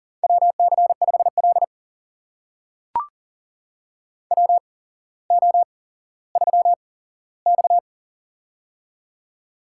Morse code, heard as a keyed tone, about 40 wpm, spaced as for 17 wpm, 700 Hz. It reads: WC5L WO3X
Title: USA Call Signs - 4 Characters - 8x Character Spacing - 40wpm